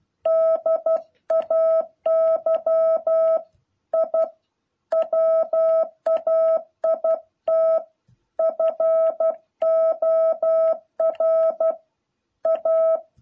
Morse code, heard as a keyed tone, about 12 wpm, 650 Hz. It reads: DAY I WAIT FOR A
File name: jh8cbh / 2021cw